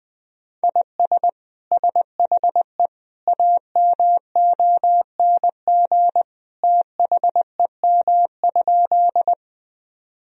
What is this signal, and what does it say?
Morse code, heard as a keyed tone, about 20 wpm, 700 Hz. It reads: IS SHE AMONG THEM?
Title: Sentences from Top 1000 Words - Review Part 3 - 20wpm